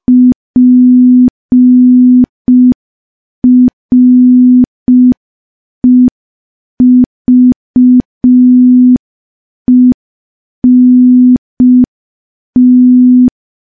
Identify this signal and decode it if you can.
Morse code, heard as a keyed tone, about 5 wpm, 250 Hz. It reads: PREVENT
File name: morse / morse-training-05WPM